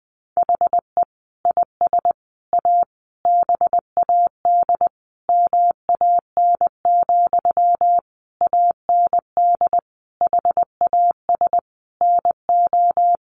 Morse code, heard as a keyed tone, about 19 wpm, 700 Hz. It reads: HE IS A BAD MAN, AND HAS NO
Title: Sentences from Top 1100 Words 20wpm